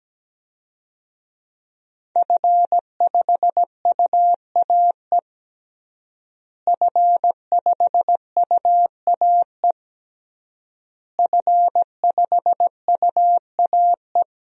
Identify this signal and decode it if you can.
Morse code, heard as a keyed tone, about 17 wpm, 700 Hz. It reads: F5UAE F5UAE F5UAE
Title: DXCC-100 Call Signs - 5 Characters - Repeated 3x - 17wpm